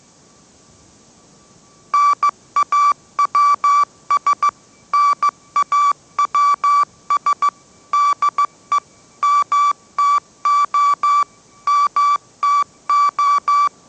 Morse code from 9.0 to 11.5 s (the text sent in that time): MTO